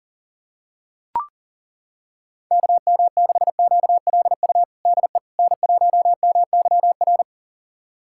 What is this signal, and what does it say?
Morse code, heard as a keyed tone, about 40 wpm, 700 Hz. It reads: KM6QLU DE N1MYR
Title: USA Callsign DE Callsign 40wpm